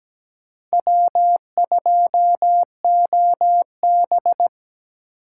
Morse code, heard as a keyed tone, about 17 wpm, 700 Hz. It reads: W2OB
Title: USA Call Signs - 4 Characters - 6x Character Spacing - 17wpm